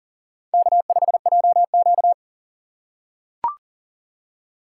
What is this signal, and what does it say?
Morse code, heard as a keyed tone, about 40 wpm, 700 Hz. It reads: K5JQ
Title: USA Call Signs - 4x Character Spacing - 40wpm